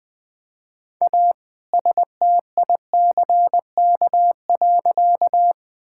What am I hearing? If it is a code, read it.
Morse code, heard as a keyed tone, about 20 wpm, 700 Hz. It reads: A STICK.
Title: All-Age Appropriate Jokes - 20wpm